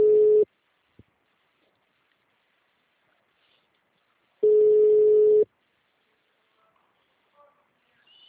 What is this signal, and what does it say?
Ringback tone (Europe, 425 Hz).